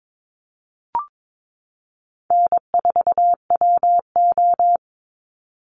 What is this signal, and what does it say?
Morse code, heard as a keyed tone, about 22 wpm, 700 Hz. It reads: N4WO
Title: Call Signs Repeated 3x - 22wpm